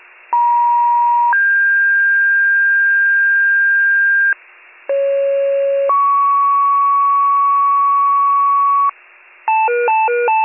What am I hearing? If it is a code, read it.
Two-tone page, heard as a one-second tone, then a three-second tone, about 950 then 1600 Hz, about 550 then 1120 Hz, with noise. It is heard from 0.3 to 8.9 s.